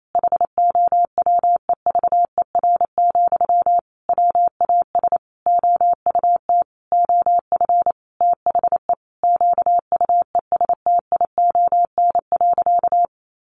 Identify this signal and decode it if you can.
Morse code, heard as a keyed tone, about 28 wpm, 700 Hz. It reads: HOWEVER, WAS OUT OF THE QUESTION.
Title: Sentences from Top 600 Words 28wpm